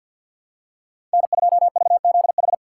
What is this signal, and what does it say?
Morse code, heard as a keyed tone, about 52 wpm, 700 Hz. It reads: N2VZH
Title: USA Call Signs - 5 Characters - 3x Character Spacing - 50wpm